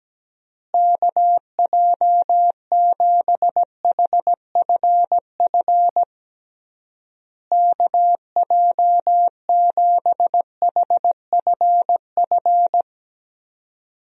Morse code, heard as a keyed tone, about 17 wpm, 700 Hz. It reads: KJ7HFF KJ7HFF
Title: USA Call Signs - 6 Characters - Repeated 3x - 17wpm